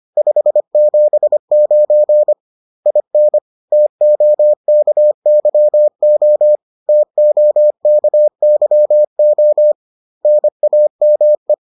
Morse code, heard as a keyed tone, about 25 wpm, 600 Hz. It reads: 579 IN TOKYO TOKYO NAME